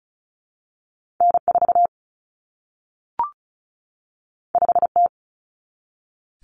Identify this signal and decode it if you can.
Morse code, heard as a keyed tone, about 36 wpm, 700 Hz. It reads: N4 5T
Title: Common Cut Numbers - 2-Digit 35wpm